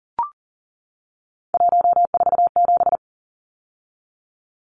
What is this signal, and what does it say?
Morse code, heard as a keyed tone, about 40 wpm, 700 Hz. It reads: 147
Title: Numbers - 3-Digit 40wpm